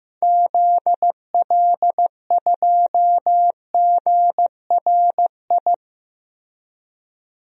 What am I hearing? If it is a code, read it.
Morse code, heard as a keyed tone, about 16 wpm, 700 Hz. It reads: ZL2GRI
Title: DXCC-100 Call Signs - 6 Characters - 15wpm